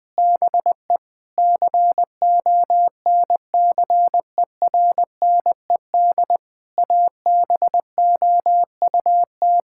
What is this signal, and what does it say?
Morse code, heard as a keyed tone, about 21 wpm, 700 Hz. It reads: BE CONCERNED ABOUT
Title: Sentences from Top 1000 Words - Review Part 6 - 20wpm